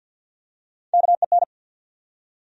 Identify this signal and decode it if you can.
Morse code, heard as a keyed tone, about 52 wpm, 700 Hz. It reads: KEN
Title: USA Common Names - Max 3 Letters - 50wpm